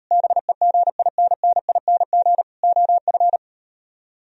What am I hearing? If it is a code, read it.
Morse code, heard as a keyed tone, about 38 wpm, 700 Hz. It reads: BEGINNING OF